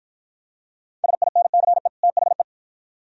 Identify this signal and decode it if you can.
Morse code, heard as a keyed tone, about 48 wpm, 700 Hz. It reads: SINCE THE